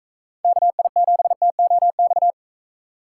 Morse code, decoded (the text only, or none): KI7TOX